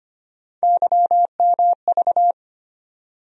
Morse code, heard as a keyed tone, about 25 wpm, 700 Hz. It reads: YMV
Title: USA Call Sign Postfixes - 3 Letters 25wpm